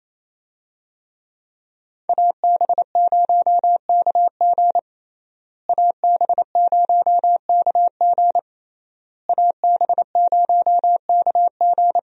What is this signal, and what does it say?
Morse code, heard as a keyed tone, about 27 wpm, 700 Hz. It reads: AB0KG AB0KG AB0KG
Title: USA Call Signs - 5 Characters - Repeated 3x - 28wpm